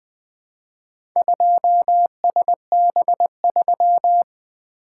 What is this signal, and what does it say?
Morse code, heard as a keyed tone, about 20 wpm, 700 Hz. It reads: 2SB3
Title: Four Letter-Number Combinations - 20wpm